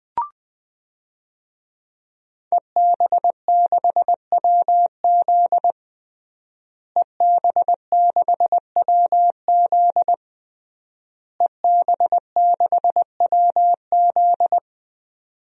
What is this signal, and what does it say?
Morse code, heard as a keyed tone, about 20 wpm, 700 Hz. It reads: EB6WZ EB6WZ EB6WZ
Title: DXCC-100 Call Signs - 5 Characters - Repeated 3x - 20wpm